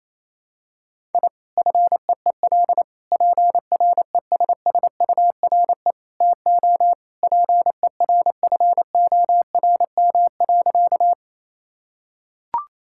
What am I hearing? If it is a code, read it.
Morse code, heard as a keyed tone, about 27 wpm, 700 Hz. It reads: I FEEL PRESSURE TO PERFORM.
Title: Sentences from Top 1000 Words - Review Part 6 - 28wpm